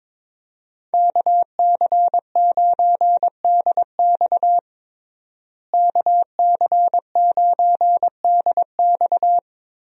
Morse code, heard as a keyed tone, about 22 wpm, 700 Hz. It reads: KC9DX KC9DX
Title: USA Call Signs - 5 Characters - Repeated 2x - 22wpm